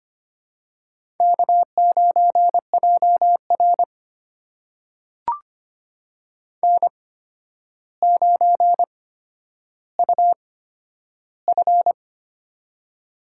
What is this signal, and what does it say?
Morse code, heard as a keyed tone, about 25 wpm, spaced as for 11 wpm, 700 Hz. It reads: K9JR N9UF
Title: USA Call Signs - 4 Characters - 8x Character Spacing - 25wpm